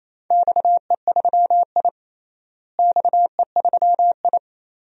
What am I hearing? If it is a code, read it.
Morse code, heard as a keyed tone, about 28 wpm, 700 Hz. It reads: XE3I XE3I
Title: DXCC-100 Call Signs - 4 Characters - Repeated 2x - 28wpm